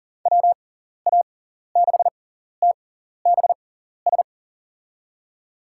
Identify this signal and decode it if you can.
Morse code, heard as a keyed tone, about 40 wpm, spaced as for 14 wpm, 700 Hz. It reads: WA6TBS
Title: USA Call Signs - 6 Characters - 6x Character Spacing - 40wpm